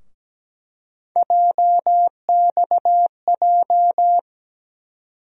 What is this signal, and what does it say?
Morse code, heard as a keyed tone, about 17 wpm, 700 Hz. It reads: JXJ
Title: USA Call Sign Postfixes - 3 Letters 17wpm